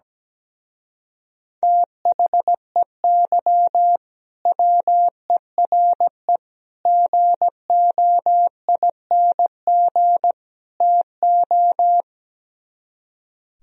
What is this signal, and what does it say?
Morse code, heard as a keyed tone, about 17 wpm, 700 Hz. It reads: THEY WERE GOING TO
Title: Speed-Racing - Sets of 4 Words 17wpm